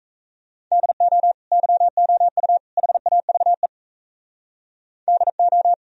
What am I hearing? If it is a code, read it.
Morse code, heard as a keyed tone, about 36 wpm, 700 Hz. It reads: DO YOU HAVE DO